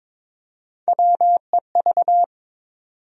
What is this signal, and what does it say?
Morse code, heard as a keyed tone, about 21 wpm, 700 Hz. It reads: WEV
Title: USA Call Sign Postfixes - 3 Letters 22wpm